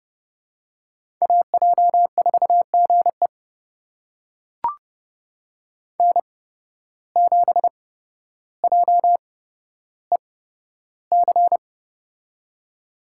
Morse code, heard as a keyed tone, about 30 wpm, spaced as for 13 wpm, 700 Hz. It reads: AJ4GE N7JEC
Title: USA Call Signs - 5 Characters - 8x Character Spacing - 30wpm